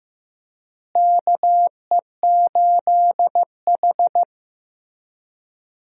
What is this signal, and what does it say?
Morse code, heard as a keyed tone, about 15 wpm, 700 Hz. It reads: KE8H
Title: USA Call Signs - 4 Characters - Repeated 3x - 15wpm